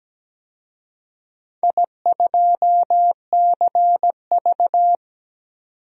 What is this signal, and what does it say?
Morse code, heard as a keyed tone, about 17 wpm, 700 Hz. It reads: I2CV
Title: DXCC-100 Call Signs - 4 Characters - 17wpm